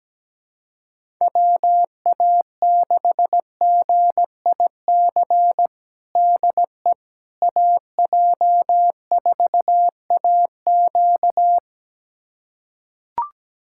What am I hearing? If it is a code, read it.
Morse code, heard as a keyed tone, about 17 wpm, 700 Hz. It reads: WA6GIC DE AJ4AQ
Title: USA Callsign DE Callsign 17wpm